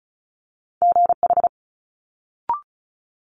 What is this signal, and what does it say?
Morse code, heard as a keyed tone, about 36 wpm, 700 Hz. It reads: GH
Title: Multi-letter Phonemes Quick Fire 35wpm